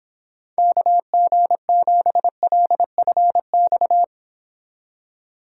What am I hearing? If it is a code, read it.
Morse code, heard as a keyed tone, about 26 wpm, 700 Hz. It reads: KG7LFX